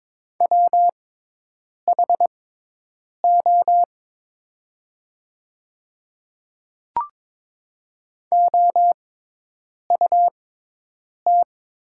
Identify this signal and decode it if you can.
Morse code, heard as a keyed tone, about 22 wpm, spaced as for 8 wpm, 700 Hz. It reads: WHO OUT
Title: 6x Character Spacing - Top 300 Words - 22wpm